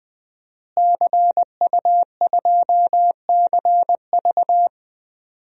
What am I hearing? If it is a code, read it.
Morse code, heard as a keyed tone, about 20 wpm, 700 Hz. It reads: CU2CV
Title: DXCC-100 Call Signs - 5 Characters - Repeated 3x - 20wpm